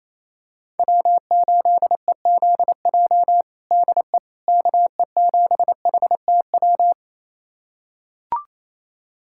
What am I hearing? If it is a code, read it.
Morse code, heard as a keyed tone, about 28 wpm, 700 Hz. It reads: W8EZJ DE KE7HTW
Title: USA Callsign DE Callsign 28wpm